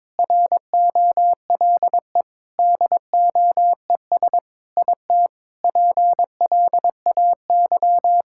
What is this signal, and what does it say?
Morse code, heard as a keyed tone, about 22 wpm, 700 Hz. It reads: ROLE DOES IT PLAY
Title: Sentences from Top 1000 Words - Review Part 6 - 22wpm